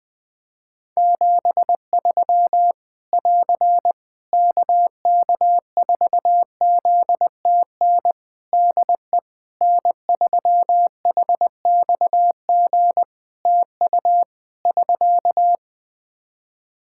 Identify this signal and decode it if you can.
Morse code, heard as a keyed tone, about 20 wpm, 700 Hz. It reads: 73 <AR> KK4ZTN DE N3HXG TU <SK>
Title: Practice QSO - Exchange 4 - Ending - 20wpm